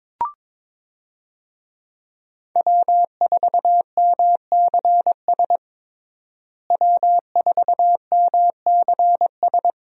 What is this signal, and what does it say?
Morse code, heard as a keyed tone, about 22 wpm, 700 Hz. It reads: W4MCS W4MCS
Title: USA Call Signs - 5 Characters - Repeated 3x - 22wpm